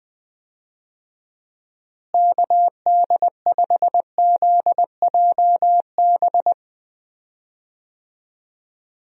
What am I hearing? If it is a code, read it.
Morse code, heard as a keyed tone, about 20 wpm, 700 Hz. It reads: KD5ZJB
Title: USA Call Signs - 6 Characters - Repeated 3x - 20wpm